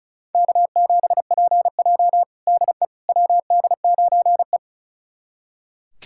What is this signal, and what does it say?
Morse code, heard as a keyed tone, about 34 wpm, 700 Hz. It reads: K7PJ DE WD9E
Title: USA Callsign DE Callsign - 4 Characters - 35wpm